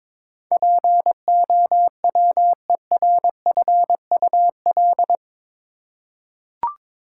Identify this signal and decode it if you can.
Morse code, heard as a keyed tone, about 21 wpm, 700 Hz. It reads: POWERFUL
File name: practice / 22wpm / Eight Letter Words - 22wpm